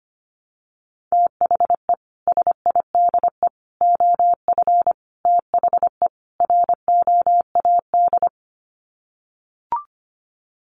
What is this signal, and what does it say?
Morse code, heard as a keyed tone, about 25 wpm, 700 Hz. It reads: THE SIDE OF THE ROAD
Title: Sets of 5 Words 25wpm